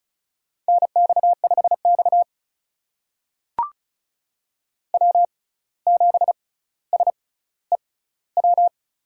Morse code, heard as a keyed tone, about 34 wpm, spaced as for 17 wpm, 700 Hz. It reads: NX5X W7SEW
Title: USA Call Signs - 6x Character Spacing - 35wpm